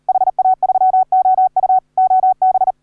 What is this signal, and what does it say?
Morse code, heard as a keyed tone, about 38 wpm, 750 Hz. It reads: HA3OU OB